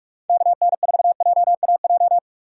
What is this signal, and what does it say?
Morse code, heard as a keyed tone, about 46 wpm, 700 Hz. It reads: KN4JAJ
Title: USA Call Signs - 6 Characters - 4x Character Spacing - 45wpm